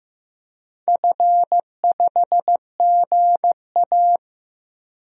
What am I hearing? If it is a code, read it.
Morse code, heard as a keyed tone, about 15 wpm, 700 Hz. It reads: F5GA
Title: DXCC-100 Call Signs - 4 Characters - Repeated 3x - 15wpm